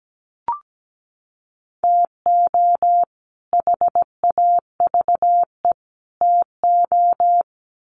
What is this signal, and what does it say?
Morse code, heard as a keyed tone, about 17 wpm, 700 Hz. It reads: TO HAVE TO
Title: Sets of 3 Words 17wpm